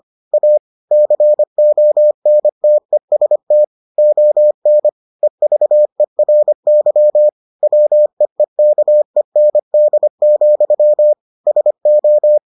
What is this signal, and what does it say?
Morse code, heard as a keyed tone, about 25 wpm, 600 Hz. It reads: A CONTEST ON EVERY WEEKEND, SO